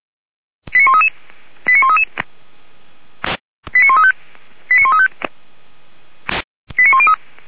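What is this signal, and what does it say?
Five-tone selective call (ZVEI), heard as quick runs of five tones, tones of about 70 ms. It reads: 89133 89133 89135 89135 89103
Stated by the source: ZVEI five-tone call received by SDR.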